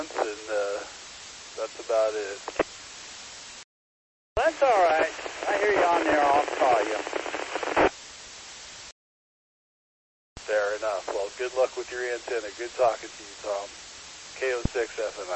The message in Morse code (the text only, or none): none